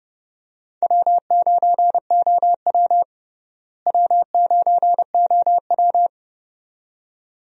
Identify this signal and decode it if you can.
Morse code, heard as a keyed tone, about 32 wpm, 700 Hz. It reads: W9OW W9OW
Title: USA Call Signs - 4 Characters - Repeated 2x - 30wpm